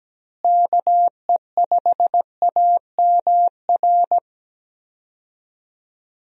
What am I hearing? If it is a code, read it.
Morse code, heard as a keyed tone, about 17 wpm, 700 Hz. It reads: KE5AMR
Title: USA Call Signs - 6 Characters - Repeated 3x - 17wpm